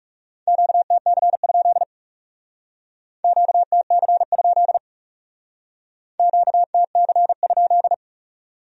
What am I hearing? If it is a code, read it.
Morse code, heard as a keyed tone, about 36 wpm, 700 Hz. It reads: QTC? QTC? QTC?
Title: Speed-Racing Q-Codes - 30wpm